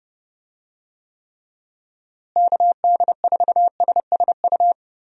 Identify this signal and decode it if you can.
Morse code, heard as a keyed tone, about 30 wpm, 700 Hz. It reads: KD4SSU
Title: USA Call Signs - 6 Characters - Repeated 3x - 30wpm